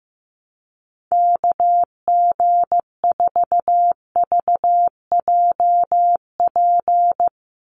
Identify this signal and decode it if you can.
Morse code, heard as a keyed tone, about 15 wpm, 700 Hz. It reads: KG4VJP
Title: USA Call Signs 15wpm